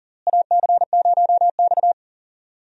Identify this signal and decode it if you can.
Morse code, heard as a keyed tone, about 40 wpm, 700 Hz. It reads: AC0X
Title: USA Call Signs - 4 Characters - Repeated 3x - 40wpm